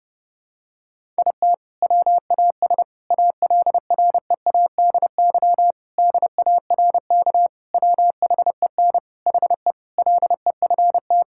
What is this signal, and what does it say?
Morse code, heard as a keyed tone, about 30 wpm, 700 Hz. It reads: IT WAS ALREADY DARK WHEN HE LEFT
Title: Sentences from Top 1000 Words - Review Part 5 - 30wpm